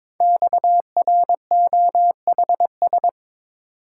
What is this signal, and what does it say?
Morse code, heard as a keyed tone, about 22 wpm, 700 Hz. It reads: XROHS
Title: Code Groups - 5 Letters - 22wpm